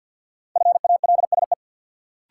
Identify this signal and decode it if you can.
Morse code, heard as a keyed tone, about 53 wpm, 700 Hz. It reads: FALSE
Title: Sentences from Top 1600 Words - New Word Drill - 50wpm